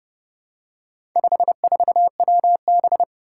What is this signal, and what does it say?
Morse code, heard as a keyed tone, about 28 wpm, 700 Hz. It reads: 54WB
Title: Four Challenging Letter-Number Combinations 30wpm